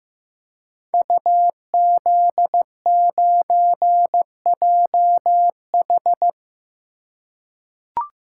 Morse code, heard as a keyed tone, about 15 wpm, 700 Hz. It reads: UZ9JH
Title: Code Groups - 5 Letter:Numbers - 15wpm